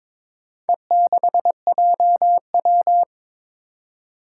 Morse code, heard as a keyed tone, about 21 wpm, 700 Hz. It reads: E6JW
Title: Four Challenging Letter-Number Combinations 22wpm